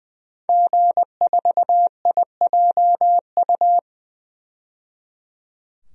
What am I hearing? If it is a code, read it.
Morse code, heard as a keyed tone, about 20 wpm, 700 Hz. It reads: G4IJU
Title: DXCC-100 Call Signs - 5 Characters - 20wpm